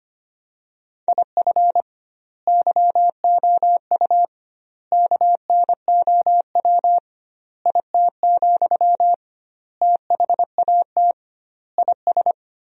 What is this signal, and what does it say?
Morse code, heard as a keyed tone, about 25 wpm, 700 Hz. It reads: IF YOU KNOW IT, THAT IS
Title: 2x Word-Spacing - Sentences from Top 400 Words - 25wpm